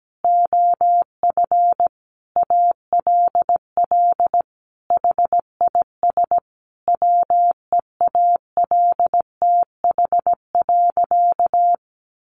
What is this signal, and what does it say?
Morse code, heard as a keyed tone, about 17 wpm, 700 Hz. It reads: OF ALL HIS WEALTH.
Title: Sentences from Top 1200 Words 17wpm